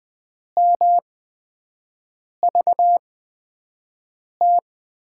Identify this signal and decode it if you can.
Morse code, heard as a keyed tone, about 20 wpm, spaced as for 5 wpm, 700 Hz. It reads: MVT
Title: USA Call Signs - 6 Characters - 8x Character Spacing - 20wpm